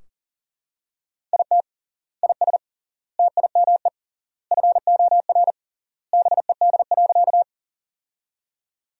Morse code, heard as a keyed tone, about 40 wpm, 700 Hz. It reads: IT IS TIME FOR BED.
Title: 3x Word-Spacing - Sentences from Top 500 Words - 40wpm